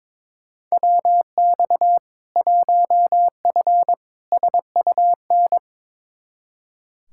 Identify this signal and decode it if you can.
Morse code, heard as a keyed tone, about 21 wpm, 700 Hz. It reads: WX 1F SUN
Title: Practice QSO - Exchange 2 - Info - 22wpm